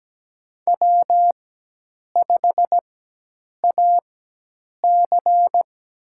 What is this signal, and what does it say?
Morse code, heard as a keyed tone, about 17 wpm, spaced as for 8 wpm, 700 Hz. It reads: W5AC
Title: USA Call Signs - 4x Character Spacing - 17wpm